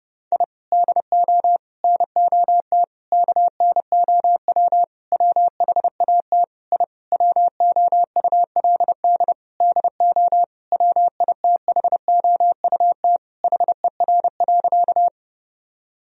Morse code, heard as a keyed tone, about 30 wpm, 700 Hz. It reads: I DO NOT KNOW WHAT I WOULD DO WITHOUT HER.
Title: Sentences from Top 1000 Words - Review Part 3 - 30wpm